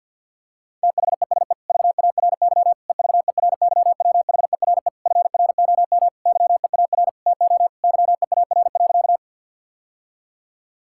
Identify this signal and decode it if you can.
Morse code, heard as a keyed tone, about 50 wpm, 700 Hz. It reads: THESE VARY EVERYWHERE FROM YEAR TO YEAR.
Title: Sentences from Top 1600 Words 50wpm